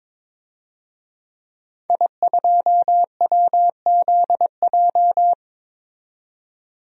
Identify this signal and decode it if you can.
Morse code, heard as a keyed tone, about 22 wpm, 700 Hz. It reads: I2WZJ
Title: DXCC-100 Call Signs - 5 Characters - 22wpm